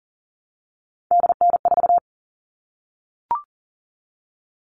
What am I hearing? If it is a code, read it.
Morse code, heard as a keyed tone, about 40 wpm, 700 Hz. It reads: DN4
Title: All Cut Numbers - 3-Digit 40wpm